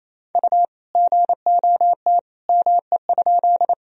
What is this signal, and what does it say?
Morse code, heard as a keyed tone, about 28 wpm, 700 Hz. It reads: U GOT ME?
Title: Sentences from Top 300 Words and Abbreviations 28wpm